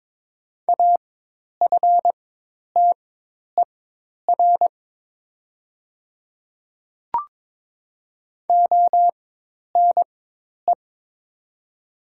Morse code, heard as a keyed tone, about 22 wpm, spaced as for 10 wpm, 700 Hz. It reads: AFTER ONE
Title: 4x Character Spacing - Top 200 Words - 22wpm